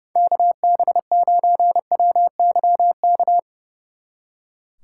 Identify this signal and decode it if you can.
Morse code, heard as a keyed tone, about 37 wpm, 700 Hz. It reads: KB9WYK
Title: Speed Racing - USA 2x3 Call Signs 30wpm